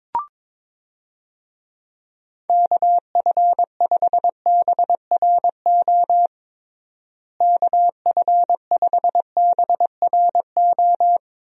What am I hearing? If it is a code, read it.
Morse code, heard as a keyed tone, about 22 wpm, 700 Hz. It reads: KF5BRO KF5BRO
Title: USA Call Signs - 6 Characters - Repeated 3x - 22wpm